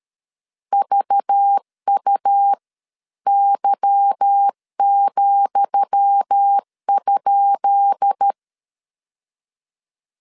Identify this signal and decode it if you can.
Morse code, heard as a keyed tone, about 13 wpm, spaced as for 11 wpm, 800 Hz. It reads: VUY,?
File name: LEZIONI / 10 SIGNS / 10D COMA 12wpm 2544